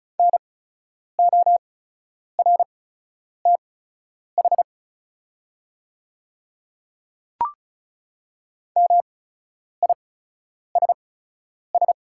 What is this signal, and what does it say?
Morse code, heard as a keyed tone, about 34 wpm, spaced as for 10 wpm, 690 Hz. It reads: NORTH MISS
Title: 8x Character Spacing - Top 500 Words - 35wpm